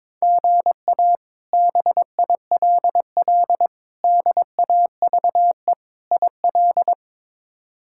Morse code, heard as a keyed tone, about 21 wpm, 700 Hz. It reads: GA BILL DAVE IL
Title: K1USN SST QSO Practice - 22wpm